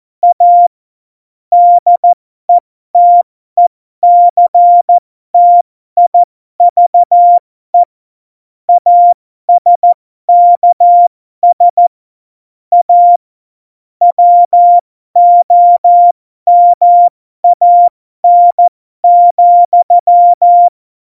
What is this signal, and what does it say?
Morse code, heard as a keyed tone, about 14 wpm, spaced as for 12 wpm, 700 Hz. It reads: A DETECTIVE ASKS A WOMAN,
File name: Morsekurs / Witze